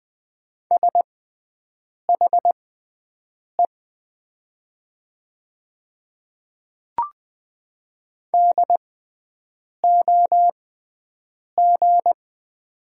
Morse code, heard as a keyed tone, about 20 wpm, spaced as for 7 wpm, 700 Hz. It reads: SHE DOG
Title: 6x Character Spacing - Top 500 Words - 20wpm